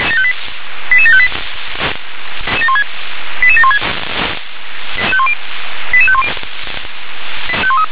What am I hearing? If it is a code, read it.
Five-tone selective call (ZVEI), heard as quick runs of five tones, tones of about 70 ms. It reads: 88758 88716 88510